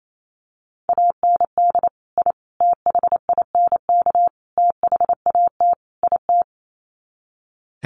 Morse code, heard as a keyed tone, about 28 wpm, 700 Hz. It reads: AND I THINK THAT IT
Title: Sets of 5 Words Encore2 28wpm